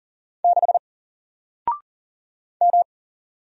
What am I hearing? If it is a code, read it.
Morse code, heard as a keyed tone, about 39 wpm, 700 Hz. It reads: 6 M
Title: Single Letter-Number Alt 40wpm